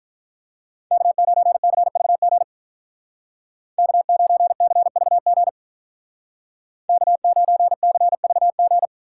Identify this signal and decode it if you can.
Morse code, heard as a keyed tone, about 43 wpm, 700 Hz. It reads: K9CVG K9CVG K9CVG
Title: Speed Racing - USA Call Signs 35wpm